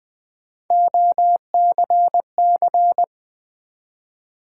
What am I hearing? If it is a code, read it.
Morse code, heard as a keyed tone, about 20 wpm, 700 Hz. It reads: OCC
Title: Build A Word - 1 Letter At A Time - 5-Letter Words - 20wpm